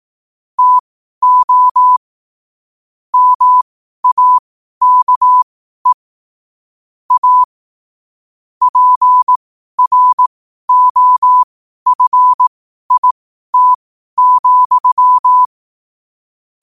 Morse code, heard as a keyed tone, about 18 wpm, spaced as for 12 wpm, 1000 Hz. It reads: TO MAKE A PROFIT,
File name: qotdfiles / 12wpm